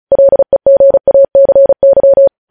Morse code, heard as a keyed tone, about 36 wpm, 550 Hz. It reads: LEGACY
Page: news headlines in morse code at 35 wpm